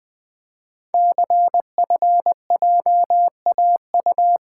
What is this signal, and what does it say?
Morse code, heard as a keyed tone, about 20 wpm, 700 Hz. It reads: CFJAU
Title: Code Groups - 5 Letters - 20wpm